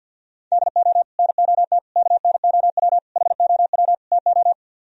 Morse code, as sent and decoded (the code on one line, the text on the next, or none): -.. --- / -. --- - / -.- -. --- .-- / .... --- .-- / - ---
DO NOT KNOW HOW TO